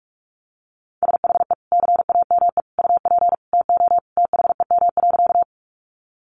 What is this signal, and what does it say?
Morse code, heard as a keyed tone, about 44 wpm, 700 Hz. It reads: SHE CAME UP TO THEM.
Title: Sentences from Top 200 Words 45wpm